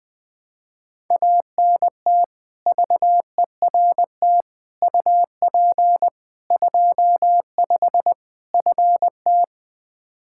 Morse code, heard as a keyed tone, about 21 wpm, 700 Hz. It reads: ANT VERT UP 25 FT
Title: K1USN SST QSO Practice 20wpm